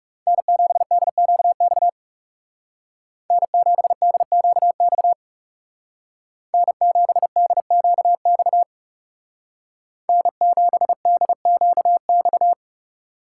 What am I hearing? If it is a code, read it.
Morse code, heard as a keyed tone, about 36 wpm, 700 Hz. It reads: N7DQX N7DQX N7DQX N7DQX
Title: Speed Racing - USA 1x3 Call Signs 30wpm